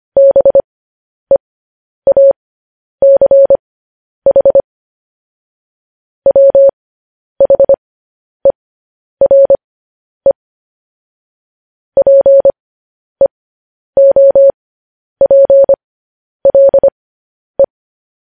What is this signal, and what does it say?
Morse code, heard as a keyed tone, about 25 wpm, spaced as for 10 wpm, 550 Hz. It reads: BEACH WHERE PEOPLE
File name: podcast / News headlines-1x-25x10